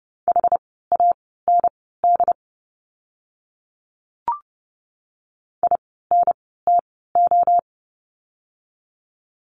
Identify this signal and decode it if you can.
Morse code, heard as a keyed tone, about 30 wpm, spaced as for 17 wpm, 700 Hz. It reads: HAND INTO
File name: course / 30:10wpm / 237 Lesson 38f - Letter q - Words 30wpm